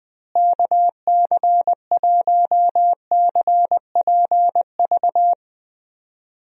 Morse code, heard as a keyed tone, about 20 wpm, 700 Hz. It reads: KC1CPV
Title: USA Call Signs - 6 Characters - Repeated 3x - 20wpm